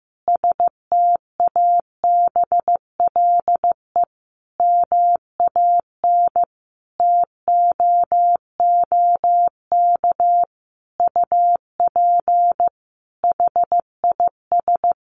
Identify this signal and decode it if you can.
Morse code, heard as a keyed tone, about 15 wpm, 700 Hz. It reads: STABLE MAN TOOK UP HIS
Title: Sentences from Top 1500 Words 15wpm